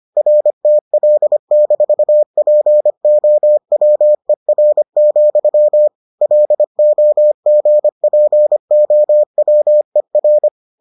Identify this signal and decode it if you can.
Morse code, heard as a keyed tone, about 25 wpm, 600 Hz. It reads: RTL-POWER, LOGPOWER